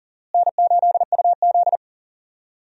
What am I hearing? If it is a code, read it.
Morse code, heard as a keyed tone, about 40 wpm, 700 Hz. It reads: N8UZ
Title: USA Call Signs - 4 Characters - Repeated 3x - 40wpm